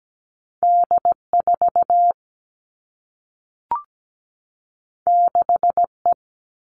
Morse code, heard as a keyed tone, about 17 wpm, 700 Hz. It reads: D4 6E
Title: All Cut Numbers - 2-Digit 17wpm